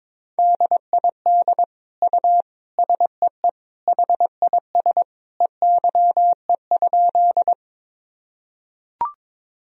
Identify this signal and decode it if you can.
Morse code, heard as a keyed tone, about 22 wpm, 700 Hz. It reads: DID U SEE HIS EYE?
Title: Sentences from Top 300 Words and Abbreviations 22wpm